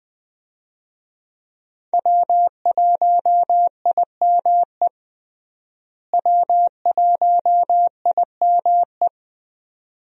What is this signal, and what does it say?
Morse code, heard as a keyed tone, about 21 wpm, 700 Hz. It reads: W1IME W1IME
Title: USA Call Signs - 5 Characters - Repeated 2x - 20wpm